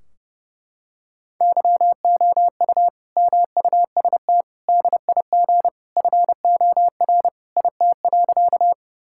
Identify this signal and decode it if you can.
Morse code, heard as a keyed tone, about 30 wpm, 700 Hz. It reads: YOU MUST DIG FOR IT.
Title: Sentences from Top 1600 Words 30wpm